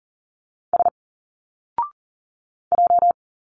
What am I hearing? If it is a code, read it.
Morse code, heard as a keyed tone, about 41 wpm, 700 Hz. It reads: S J
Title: Single Letters 40wpm